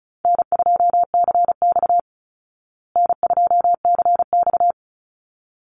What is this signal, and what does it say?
Morse code, heard as a keyed tone, about 34 wpm, 700 Hz. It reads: N2CX N2CX
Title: Call Signs Repeated 3x - 35wpm